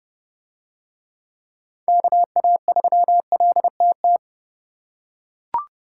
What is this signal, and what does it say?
Morse code, heard as a keyed tone, about 28 wpm, 700 Hz. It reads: KA3LTT